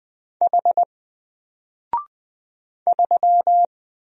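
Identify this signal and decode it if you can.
Morse code, heard as a keyed tone, about 20 wpm, 700 Hz. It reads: H 3
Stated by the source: Single letter and number Morse drill at 20 wpm, from N4LCD.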